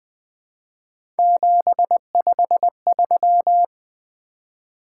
Morse code, heard as a keyed tone, about 21 wpm, 700 Hz. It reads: 753
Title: Three Letter-Number Combinations - 20wpm